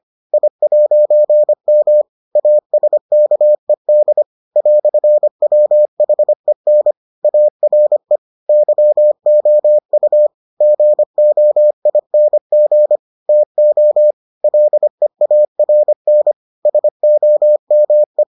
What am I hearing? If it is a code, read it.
Morse code, heard as a keyed tone, about 25 wpm, 600 Hz. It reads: I'M ASKED "WHEN ARE YOU GOING TO LEARN SOME